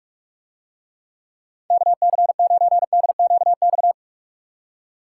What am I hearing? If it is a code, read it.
Morse code, heard as a keyed tone, about 48 wpm, 700 Hz. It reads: KC9DQX